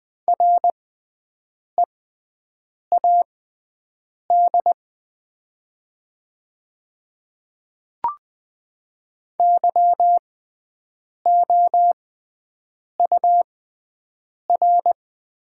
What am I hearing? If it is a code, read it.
Morse code, heard as a keyed tone, about 20 wpm, spaced as for 7 wpm, 700 Hz. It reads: READ YOUR
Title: 6x Character Spacing - Top 200 Words - 20wpm